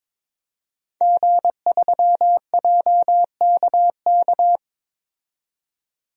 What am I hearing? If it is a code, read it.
Morse code, heard as a keyed tone, about 22 wpm, 700 Hz. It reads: G3JKK
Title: DXCC-100 Call Signs - 5 Characters - 22wpm